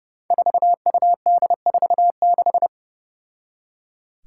Morse code, heard as a keyed tone, about 32 wpm, 700 Hz. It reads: 4UD46
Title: Five Challenging Letter-Number Combinations 30wpm